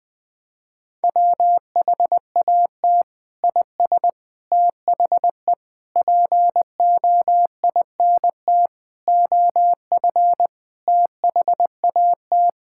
Morse code, heard as a keyed tone, about 20 wpm, 700 Hz. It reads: WHAT IS THE POINT OF THAT
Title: Sentences from Top 1000 Words - Review Part 5 - 20wpm